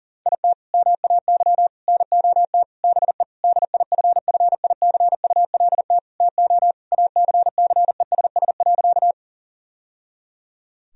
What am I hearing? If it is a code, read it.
Morse code, heard as a keyed tone, about 40 wpm, 700 Hz. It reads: IT MAY NOT BE DIFFICULT TO ACCESS.
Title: Sentences from Top 1000 Words - Review Part 4 - 40wpm